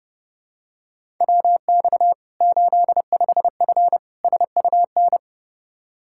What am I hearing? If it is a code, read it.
Morse code, heard as a keyed tone, about 30 wpm, 700 Hz. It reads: WX 85F SUN
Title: Practice QSO - Exchange 2 - Info - 30wpm